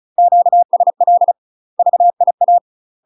Morse code, heard as a keyed tone, about 36 wpm, 700 Hz. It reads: QSL VIA
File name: UK MASTER QSOs TRAINING / 35WPM / UK MASTER QSOs 0029